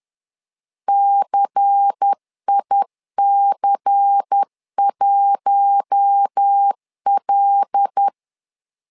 Morse code, heard as a keyed tone, about 10 wpm, 800 Hz. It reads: CIC1L
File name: LEZIONI / 5 LBVC / 5 LBVC 10wpm 2349